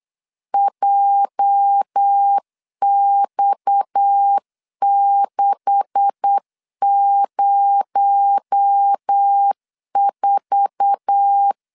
Morse code, heard as a keyed tone, about 8 wpm, 800 Hz. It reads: JX604
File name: CW / DET / 1 9 1500C S B 8wpm 5624